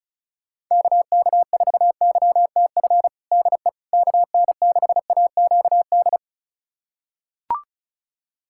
Morse code, heard as a keyed tone, about 36 wpm, 700 Hz. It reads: KK4YTF DE KN6AQD
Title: USA Callsign DE Callsign - 6 Characters - 35wpm